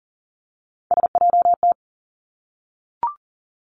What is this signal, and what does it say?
Morse code, heard as a keyed tone, about 42 wpm, 700 Hz. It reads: SJT
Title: Three Challenging Letter-Number Combinations 40wpm